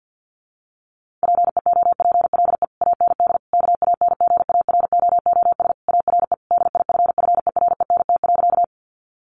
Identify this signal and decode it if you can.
Morse code, heard as a keyed tone, about 50 wpm, 700 Hz. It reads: PEOPLE AND KANGAROOS ARE DIFFERENT.
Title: Easy Phrases 50wpm